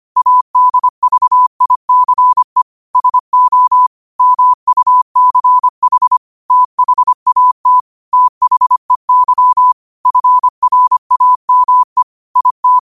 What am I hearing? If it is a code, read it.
Morse code, heard as a keyed tone, about 25 wpm, 1000 Hz. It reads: ADVICE SO MUCH THAT THEY FRAME IT
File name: qotdfiles / 25wpm